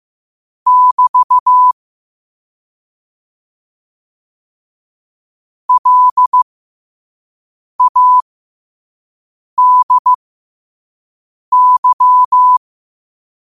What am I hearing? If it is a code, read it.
Morse code, heard as a keyed tone, about 15 wpm, spaced as for 5 wpm, 1000 Hz. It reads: <BT> LADY
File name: qotdfiles / 5wpm